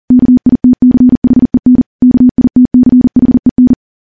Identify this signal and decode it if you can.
Morse code, heard as a keyed tone, about 40 wpm, 250 Hz. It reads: KITCHEN KITCHEN
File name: morse / morse-training-40WPM